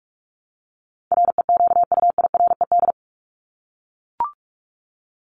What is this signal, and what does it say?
Morse code, heard as a keyed tone, about 44 wpm, 700 Hz. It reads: REQUIRED
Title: ED Suffix - Top 100 Words with Base Words 45wpm